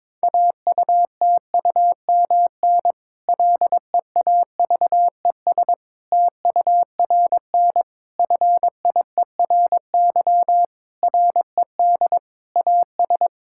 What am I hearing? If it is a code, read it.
Morse code, heard as a keyed tone, about 21 wpm, 700 Hz. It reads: AUTUMN LEAVES TURN FIERY RED AS
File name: Difficult Sentences with Hints - 22wpm